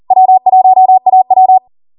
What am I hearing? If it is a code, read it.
Morse code, heard as a keyed tone, about 40 wpm, 750 Hz. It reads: W1AW